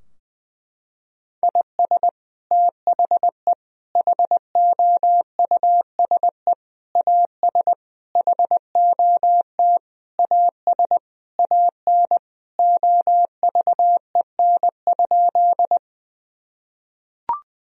Morse code, heard as a keyed tone, about 20 wpm, 700 Hz. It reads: IS THE HOUSE AS HOT AS AN OVEN?
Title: Sentences from Top 1000 Words - Review Part 4 - 20wpm